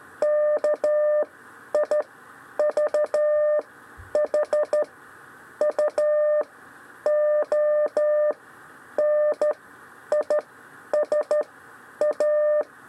Morse code, heard as a keyed tone, about 13 wpm, spaced as for 9 wpm, 600 Hz. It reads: KIVHU ONISA
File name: 6 lezione cw 1